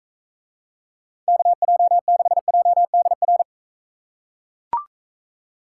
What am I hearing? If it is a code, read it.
Morse code, heard as a keyed tone, about 43 wpm, 700 Hz. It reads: KJ6JDR